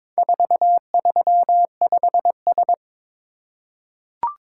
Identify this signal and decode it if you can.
Morse code, heard as a keyed tone, about 22 wpm, 700 Hz. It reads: 435S
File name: practice / 22wpm / Four Letter-Number Combinations - 22wpm